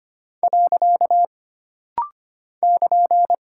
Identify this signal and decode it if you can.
Morse code, heard as a keyed tone, about 25 wpm, 700 Hz. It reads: . <KN>
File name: Single Punctuation-Prosign - Rapid-Fire - 25wpm